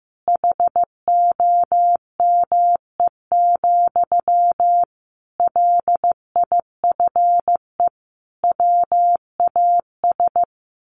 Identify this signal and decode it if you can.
Morse code, heard as a keyed tone, about 14 wpm, 700 Hz. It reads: HOME, LIFE WAS
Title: Sentences from Top 1400 Words 15wpm